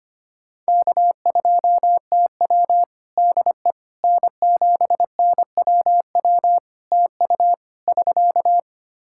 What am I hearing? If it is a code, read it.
Morse code, heard as a keyed tone, about 25 wpm, 700 Hz. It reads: K2TW DE N7NWW TU <SK>
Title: K1USN SST QSO Practice 25wpm